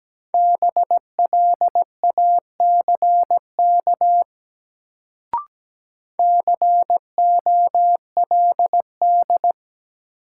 Morse code, heard as a keyed tone, about 17 wpm, 700 Hz. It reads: BLACK COLD
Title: Sentences from Top 400 Words - New Word Drill - 17wpm